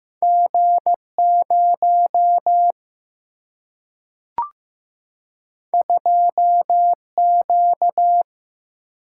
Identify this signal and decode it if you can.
Morse code, heard as a keyed tone, about 15 wpm, 700 Hz. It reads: G0 2Q
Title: Single-Letter Single-Number 15wpm